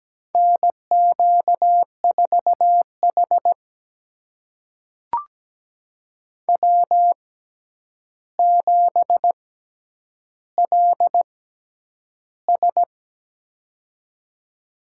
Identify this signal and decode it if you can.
Morse code, heard as a keyed tone, about 17 wpm, spaced as for 9 wpm, 700 Hz. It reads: NQ4H W7LS
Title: USA Call Signs - 4 Characters - 6x Character Spacing - 17wpm